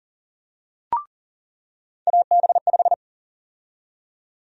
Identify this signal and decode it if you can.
Morse code, heard as a keyed tone, about 40 wpm, 700 Hz. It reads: AB5
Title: USA Call Sign Prefixes - 40wpm